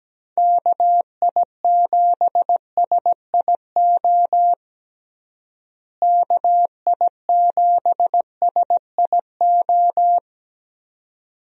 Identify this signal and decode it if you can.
Morse code, heard as a keyed tone, about 17 wpm, 700 Hz. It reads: KI7SIO KI7SIO
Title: USA Call Signs - 6 Characters - Repeated 2x - 17wpm